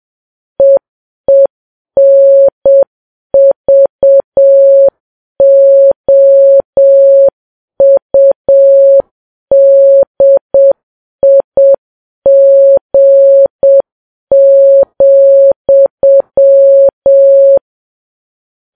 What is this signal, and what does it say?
Morse code, heard as a keyed tone, about 7 wpm, 550 Hz. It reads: EENVOUDIG,
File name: Files Web tls / Morse on9fv / morse7kps